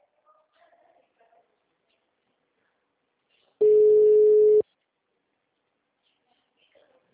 Ringback tone (Europe, 425 Hz).
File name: out- -2006- - - .5596